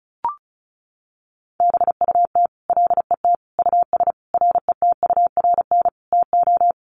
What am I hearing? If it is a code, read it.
Morse code, heard as a keyed tone, about 34 wpm, 700 Hz. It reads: BUT LET US RETURN TO
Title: Sentences from Top 800 Words 35wpm